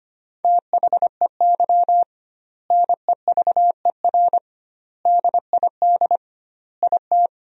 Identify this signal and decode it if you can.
Morse code, heard as a keyed tone, about 25 wpm, 700 Hz. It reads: THEY NEVER DID IT
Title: 2x Word-Spacing - Sentences from Top 300 Words - 25wpm